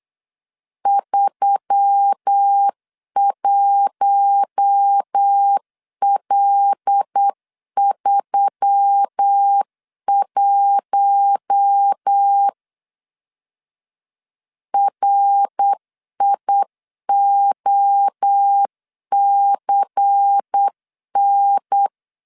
Morse code, heard as a keyed tone, about 8 wpm, 800 Hz. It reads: 31L31 RIOCN
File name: CW / DET / 1 6 1500C S C 8wpm 5219